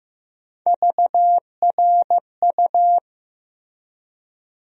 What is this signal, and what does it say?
Morse code, heard as a keyed tone, about 15 wpm, 700 Hz. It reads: VRU
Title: USA Call Sign Postfixes - 3 Letters 15wpm